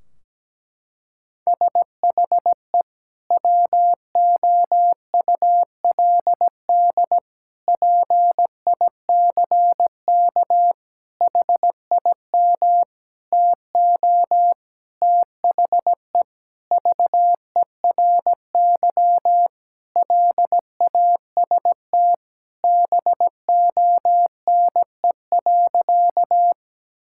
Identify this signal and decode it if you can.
Morse code, heard as a keyed tone, about 17 wpm, 700 Hz. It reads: SHE WOULD PICK HIM TO THE VERY LAST BONE.
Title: Sentences from Top 1600 Words 17wpm